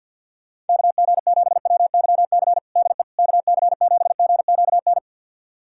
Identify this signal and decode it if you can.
Morse code, heard as a keyed tone, about 51 wpm, 690 Hz. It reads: KG7WYC DE KC7GQN
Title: USA Callsign DE Callsign 50wpm